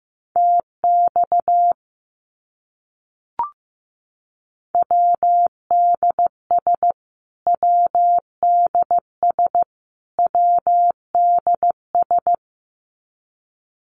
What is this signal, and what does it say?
Morse code, heard as a keyed tone, about 15 wpm, 700 Hz. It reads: TX WDS WDS WDS
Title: Abbreviations Repeated 3x - 15wpm